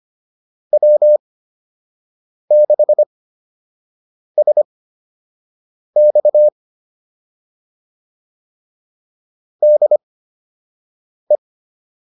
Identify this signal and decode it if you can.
Morse code, heard as a keyed tone, about 25 wpm, spaced as for 6 wpm, 600 Hz. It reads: W6SX DE
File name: QSO 102 06